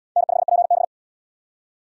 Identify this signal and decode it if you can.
Morse code, heard as a keyed tone, about 76 wpm, 700 Hz. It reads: N5PB